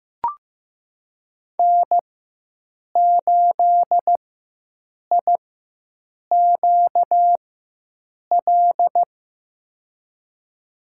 Morse code, heard as a keyed tone, about 15 wpm, spaced as for 7 wpm, 700 Hz. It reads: N8IQL
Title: USA Call Signs - 4x Character Spacing - 15wpm